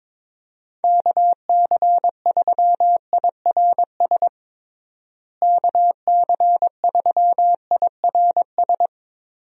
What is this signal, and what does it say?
Morse code, heard as a keyed tone, about 21 wpm, 700 Hz. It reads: KC3IRS KC3IRS
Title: USA Call Signs - 6 Characters - Repeated 3x - 22wpm